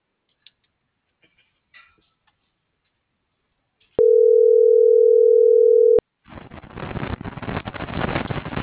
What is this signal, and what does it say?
Ringback tone (North America), 440 + 480 Hz.